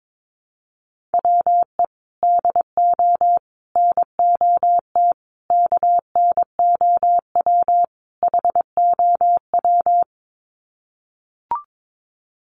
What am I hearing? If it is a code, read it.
Morse code, heard as a keyed tone, about 22 wpm, 700 Hz. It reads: WE DO NOT KNOW HOW
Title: Sets of 5 Words Encore 22wpm